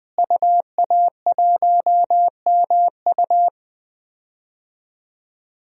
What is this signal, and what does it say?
Morse code, heard as a keyed tone, about 21 wpm, 700 Hz. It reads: UA1MU
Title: DXCC-100 Call Signs - 5 Characters - Repeated 3x - 20wpm